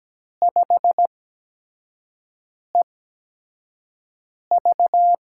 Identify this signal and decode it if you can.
Morse code, heard as a keyed tone, about 17 wpm, spaced as for 5 wpm, 700 Hz. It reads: 5EV
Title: USA Call Signs - 4 Characters - 8x Character Spacing - 17wpm